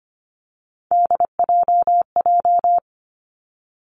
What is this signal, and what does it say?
Morse code, heard as a keyed tone, about 25 wpm, 700 Hz. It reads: DJJ
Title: Three Challenging Letter-Number Combinations 25wpm